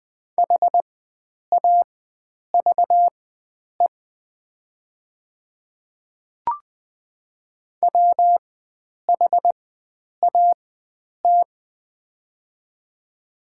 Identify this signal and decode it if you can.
Morse code, heard as a keyed tone, about 20 wpm, spaced as for 9 wpm, 700 Hz. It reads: HAVE WHAT
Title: 4x Character Spacing - Top 100 Words - 20wpm